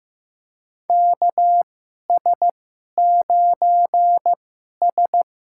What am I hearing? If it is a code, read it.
Morse code, heard as a keyed tone, about 15 wpm, spaced as for 11 wpm, 700 Hz. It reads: KS9S
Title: USA Call Signs - 4 Characters - 2x Character Spacing - 15wpm